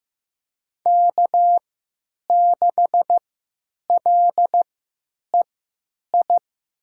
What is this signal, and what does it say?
Morse code, heard as a keyed tone, about 15 wpm, spaced as for 9 wpm, 700 Hz. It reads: K6LEI
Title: USA Call Signs - 5 Characters - 3x Character Spacing - 15wpm